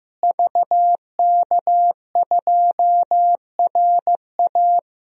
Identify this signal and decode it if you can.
Morse code, heard as a keyed tone, about 15 wpm, 700 Hz. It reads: VK2RA
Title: DXCC-100 Call Signs - 5 Characters - Repeated 3x - 15wpm